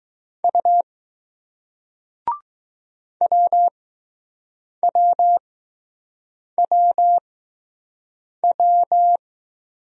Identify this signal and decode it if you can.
Morse code, heard as a keyed tone, about 20 wpm, spaced as for 6 wpm, 700 Hz. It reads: U WWWW